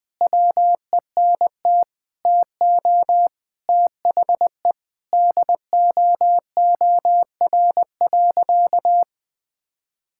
Morse code, heard as a keyed tone, about 21 wpm, 700 Hz. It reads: WENT TO THE DOOR.
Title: Sentences from Top 1600 Words 20wpm